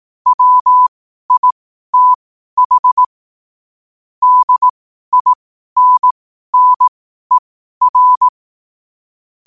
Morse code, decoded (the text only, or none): WITH DINNER